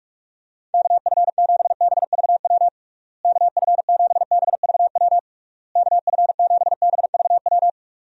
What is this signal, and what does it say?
Morse code, heard as a keyed tone, about 41 wpm, 700 Hz. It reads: KF7BVW KF7BVW KF7BVW
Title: USA Call Signs - 6 Characters - Repeated 3x - 45wpm